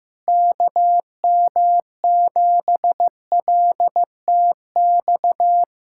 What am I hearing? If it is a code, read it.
Morse code, heard as a keyed tone, about 15 wpm, 700 Hz. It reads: KM7LTX
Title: USA Callsign DE Callsign 15wpm